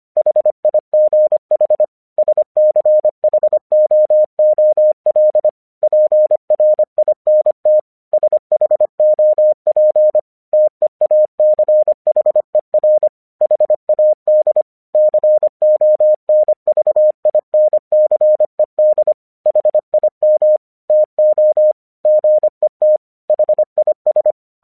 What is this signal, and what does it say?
Morse code, heard as a keyed tone, about 25 wpm, 610 Hz. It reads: HIGH SCHOOL PRINT SHOP TEACHER HAD CONVINCED HIM TO GET HIS
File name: 1 English 25wpm / AQC09